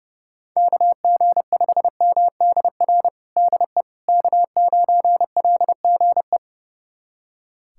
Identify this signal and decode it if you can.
Morse code, heard as a keyed tone, about 30 wpm, 700 Hz. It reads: KG5MDR DE K9LGE
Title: USA Callsign DE Callsign 30wpm